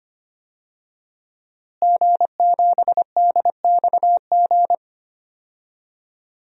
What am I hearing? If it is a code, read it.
Morse code, heard as a keyed tone, about 25 wpm, 700 Hz. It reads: G7DXG
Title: DXCC-100 Call Signs - 5 Characters - Repeated 3x - 25wpm